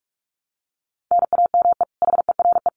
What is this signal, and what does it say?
Morse code, heard as a keyed tone, about 44 wpm, 700 Hz. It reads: NAME HERE
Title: Common QSO Phrases 45wpm